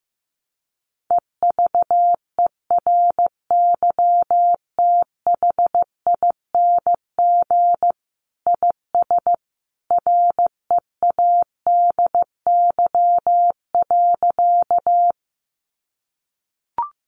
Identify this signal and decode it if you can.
Morse code, heard as a keyed tone, about 15 wpm, 700 Hz. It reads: EVERYTHING IS READY.
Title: Sentences from Top 700 Words 15wpm